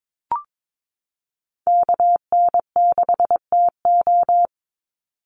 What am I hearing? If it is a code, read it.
Morse code, heard as a keyed tone, about 22 wpm, 700 Hz. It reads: KN6TO
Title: USA Call Signs 22wpm